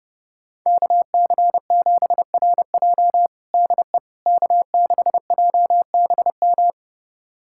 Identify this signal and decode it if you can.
Morse code, heard as a keyed tone, about 32 wpm, 700 Hz. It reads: KC7RJ DE K6JBM
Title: USA Callsign DE Callsign - 5 Characters - 30wpm